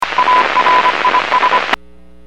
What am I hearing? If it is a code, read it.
Morse code, heard as a keyed tone, about 25 wpm, 1000 Hz, with radio noise. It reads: ARIS
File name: WPS / 00376 14-04 WPS 451.25 WPS